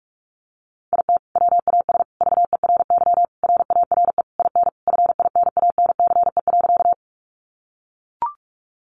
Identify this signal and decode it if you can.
Morse code, heard as a keyed tone, about 44 wpm, 700 Hz. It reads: IT WAS VERY RARE IN FINANCE.
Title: Sentences from Top 1500 Words 45wpm